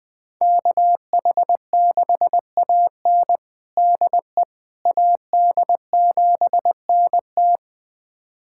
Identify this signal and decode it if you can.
Morse code, heard as a keyed tone, about 20 wpm, 700 Hz. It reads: KH6AN DE AD7NT
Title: USA Callsign DE Callsign 20wpm